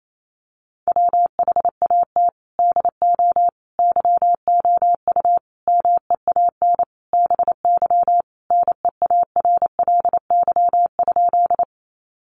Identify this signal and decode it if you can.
Morse code, heard as a keyed tone, about 28 wpm, 700 Hz. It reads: WHAT DO YOU MEAN BY NEARLY?
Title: Sentences from Top 1300 Words 28wpm